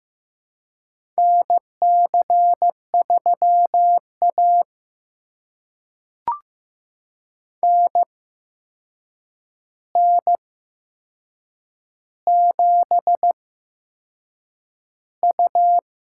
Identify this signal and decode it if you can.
Morse code, heard as a keyed tone, about 15 wpm, spaced as for 6 wpm, 700 Hz. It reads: NC3A NN7U
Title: USA Call Signs - 4 Characters - 8x Character Spacing - 15wpm